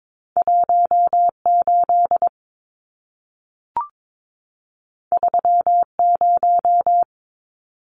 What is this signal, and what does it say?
Morse code, heard as a keyed tone, about 22 wpm, 700 Hz. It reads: 18 30
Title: Numbers - 2-Digit 22wpm